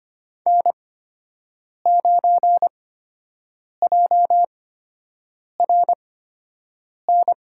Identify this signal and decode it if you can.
Morse code, heard as a keyed tone, about 26 wpm, spaced as for 7 wpm, 700 Hz. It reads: N9JRN
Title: USA Call Signs - 5 Characters - 8x Character Spacing - 25wpm